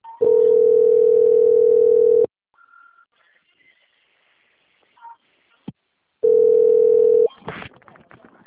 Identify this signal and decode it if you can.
Ringback tone (North America), 440 + 480 Hz.